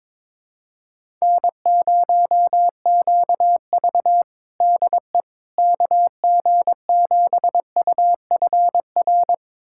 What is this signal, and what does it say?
Morse code, heard as a keyed tone, about 22 wpm, 700 Hz. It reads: N0QV DE KG7UFR
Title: USA Callsign DE Callsign 22wpm